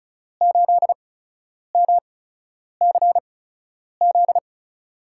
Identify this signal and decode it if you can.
Morse code, heard as a keyed tone, about 34 wpm, spaced as for 10 wpm, 700 Hz. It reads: 8MCZ
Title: USA Call Signs - 6 Characters - 8x Character Spacing - 35wpm